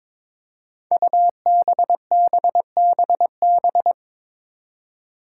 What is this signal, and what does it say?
Morse code, heard as a keyed tone, about 21 wpm, 700 Hz. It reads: UBBBB
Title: Five Challenging Letter-Number Combinations 22wpm